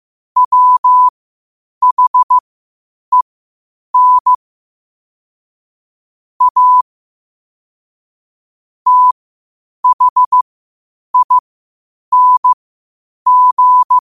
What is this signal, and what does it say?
Morse code, heard as a keyed tone, about 15 wpm, spaced as for 8 wpm, 1000 Hz. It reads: WHEN A THING
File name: qotdfiles / 8wpm